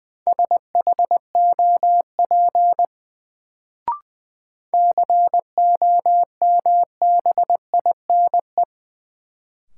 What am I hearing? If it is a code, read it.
Morse code, heard as a keyed tone, about 20 wpm, 700 Hz. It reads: SHOP COMBINE
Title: Sentences from Top 1600 Words - New Word Drill - 20wpm